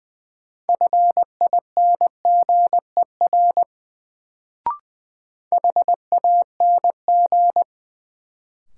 Morse code, heard as a keyed tone, about 20 wpm, 700 Hz. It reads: FINGER HANG
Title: Sentences from Top 1600 Words - New Word Drill Alt - 20wpm